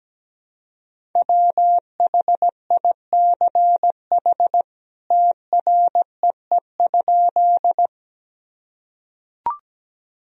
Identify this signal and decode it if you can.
Morse code, heard as a keyed tone, about 17 wpm, 700 Hz. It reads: WHICH TREE?
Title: Sentences from Top 1000 Words - Review Part 1 - 17wpm